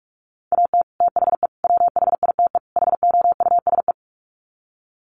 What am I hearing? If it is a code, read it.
Morse code, heard as a keyed tone, about 44 wpm, 700 Hz. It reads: AT THE WHITE HOUSE
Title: Sets of 4 Words 45wpm